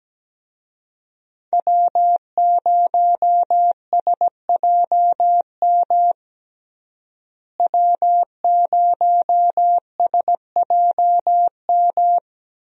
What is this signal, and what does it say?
Morse code, heard as a keyed tone, about 17 wpm, 700 Hz. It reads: W0SJM W0SJM
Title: USA Call Signs - 5 Characters - Repeated 3x - 17wpm